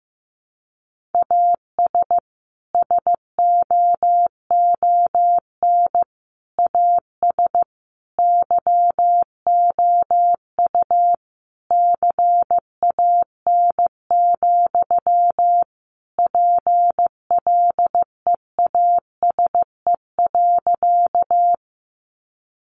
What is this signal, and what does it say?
Morse code, heard as a keyed tone, about 15 wpm, 700 Hz. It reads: AS SOON AS YOU CAN, PLEASE.
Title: Sentences from Top 1500 Words 15wpm